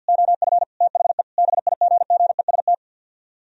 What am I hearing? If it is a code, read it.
Morse code, heard as a keyed tone, about 64 wpm, 700 Hz. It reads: OF THE BIGGEST